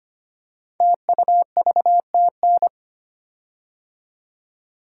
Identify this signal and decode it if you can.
Morse code, heard as a keyed tone, about 25 wpm, 700 Hz. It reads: TUVTN
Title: Code Groups - 5 Letters - 25wpm